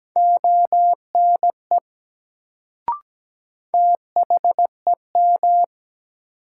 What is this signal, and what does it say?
Morse code, heard as a keyed tone, about 17 wpm, 700 Hz. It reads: ONE THEM
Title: Sets of 2 Words - New Word Drill - 17wpm